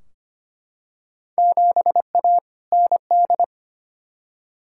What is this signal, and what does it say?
Morse code, heard as a keyed tone, about 25 wpm, 700 Hz. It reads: 7A ND
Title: Field Day Exchanges Encore - 25wpm